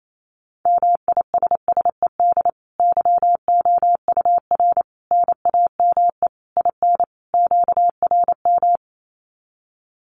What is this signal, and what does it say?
Morse code, heard as a keyed tone, about 28 wpm, 700 Hz. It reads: MISSED YOUR NAME IN QRM
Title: Common QSO Phrases 28wpm